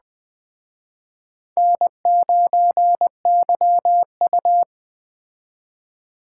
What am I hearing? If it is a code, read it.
Morse code, heard as a keyed tone, about 21 wpm, 700 Hz. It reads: N9YU